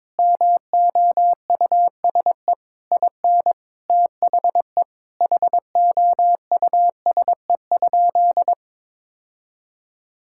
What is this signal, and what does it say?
Morse code, heard as a keyed tone, about 22 wpm, 700 Hz. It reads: MOUSE IN THE HOUSE?
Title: Sentences from Top 1000 Words - Review Part 5 - 22wpm